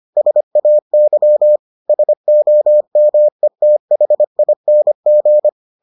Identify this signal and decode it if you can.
Morse code, heard as a keyed tone, about 25 wpm, 600 Hz. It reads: SAY SOMETHING